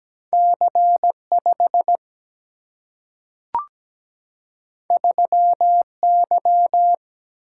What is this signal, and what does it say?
Morse code, heard as a keyed tone, about 17 wpm, 700 Hz. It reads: C5 3Y
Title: Single-Letter Single-Number 17wpm